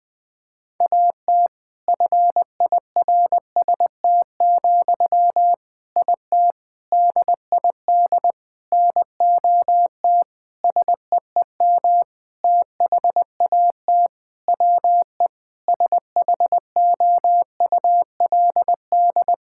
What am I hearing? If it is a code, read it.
Morse code, heard as a keyed tone, about 20 wpm, 700 Hz. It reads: AT FIRST, IT DID NOT SEEM THAT WE SHOULD
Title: Sentences from Top 1000 Words - Review Part 6 - 20wpm